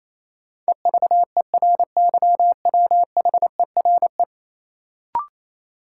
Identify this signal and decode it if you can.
Morse code, heard as a keyed tone, about 28 wpm, 700 Hz. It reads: EVERYWHERE
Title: Sentences from Top 1600 Words - New Word Drill - 28wpm